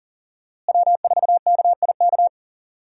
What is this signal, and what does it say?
Morse code, heard as a keyed tone, about 40 wpm, 700 Hz. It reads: W4KIK
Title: USA Call Signs - 5 Characters - Repeated 2x - 40wpm